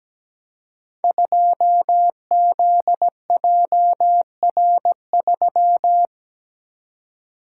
Morse code, heard as a keyed tone, about 17 wpm, 700 Hz. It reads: 2ZJR3
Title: Code Groups - 5 Letter:Numbers - 17wpm